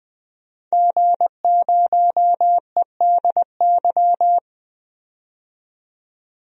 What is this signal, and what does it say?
Morse code, heard as a keyed tone, about 20 wpm, 700 Hz. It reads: G0EDY
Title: DXCC-100 Call Signs - 5 Characters - 20wpm